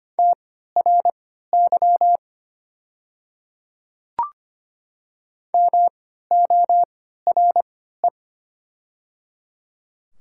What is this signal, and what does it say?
Morse code, heard as a keyed tone, about 25 wpm, spaced as for 14 wpm, 700 Hz. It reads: TRY MORE
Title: 3x Character Spacing - Top 300 Words - 25wpm